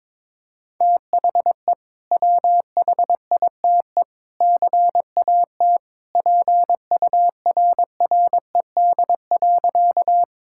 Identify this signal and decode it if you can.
Morse code, heard as a keyed tone, about 21 wpm, 700 Hz. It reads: THE WHITE CAT PURRED.
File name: practice / 22wpm / Sentences with 2-Word Phrases - 22wpm